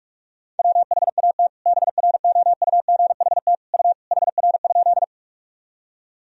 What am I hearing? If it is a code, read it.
Morse code, heard as a keyed tone, about 48 wpm, 700 Hz. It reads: WHAT BROUGHT U HR?
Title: Sentences from Top 500 Words and Abbreviations 45wpm